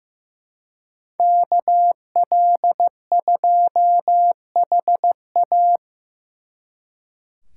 Morse code, heard as a keyed tone, about 18 wpm, 700 Hz. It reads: KL2HA
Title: Speed Racing - USA 2x2 Call Signs 15wpm